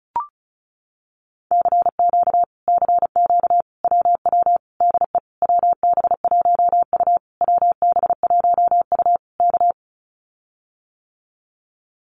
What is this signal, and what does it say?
Morse code, heard as a keyed tone, about 34 wpm, 700 Hz. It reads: CQ CQ WW DE WB1U WB1U K
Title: Common QSO Phrases Encore 35wpm